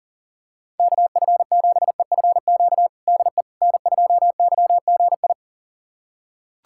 Morse code, heard as a keyed tone, about 40 wpm, 700 Hz. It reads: KF7EFQ DE N2YGI
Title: USA Callsign DE Callsign 40wpm